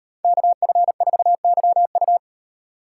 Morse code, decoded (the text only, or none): KF4YU